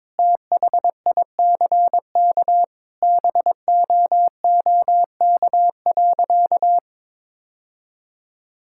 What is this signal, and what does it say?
Morse code, heard as a keyed tone, about 22 wpm, 700 Hz. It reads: THICK BOOK.
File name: practice / 22wpm / Sentences with 4-Word Phrases - 22wpm